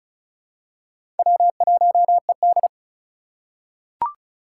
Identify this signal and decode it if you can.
Morse code, heard as a keyed tone, about 36 wpm, 700 Hz. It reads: W1ED
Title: USA Call Signs - 4 Characters - 3x Character Spacing - 35wpm